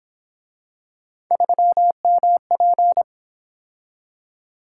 Morse code, heard as a keyed tone, about 26 wpm, 700 Hz. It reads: 3MP
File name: Speed-Racing - Three Letter-Number Combinations - 22wpm